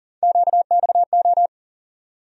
Morse code, heard as a keyed tone, about 40 wpm, 700 Hz. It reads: QXO
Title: USA Call Sign Postfixes - 3 Letters 40wpm